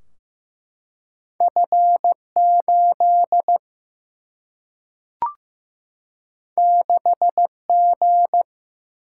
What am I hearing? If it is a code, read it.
Morse code, heard as a keyed tone, about 15 wpm, 700 Hz. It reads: F8 6G
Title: Single-Letter Single-Number 15wpm